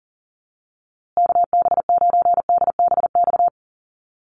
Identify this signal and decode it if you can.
Morse code, heard as a keyed tone, about 40 wpm, 700 Hz. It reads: KB9DBX
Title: USA Call Signs Repeated 3x 40wpm